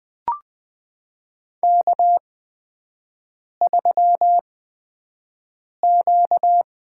Morse code, heard as a keyed tone, about 20 wpm, spaced as for 5 wpm, 700 Hz. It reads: K3Q
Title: USA Call Signs - 8x Character Spacing - 20wpm